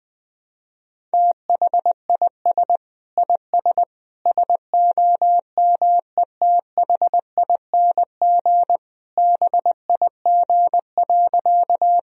Morse code, heard as a keyed tone, about 20 wpm, 700 Hz. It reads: THIS IS SOMETHING BIG.
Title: Sentences from Top 1000 Words - Review Part 1 - 20wpm